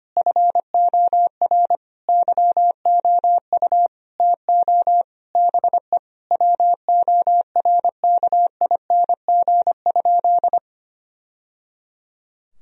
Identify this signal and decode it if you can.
Morse code, heard as a keyed tone, about 25 wpm, 700 Hz. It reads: FOR YOU TO BE WORKING?
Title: Sentences from Top 1000 Words - Review Part 4 - 25wpm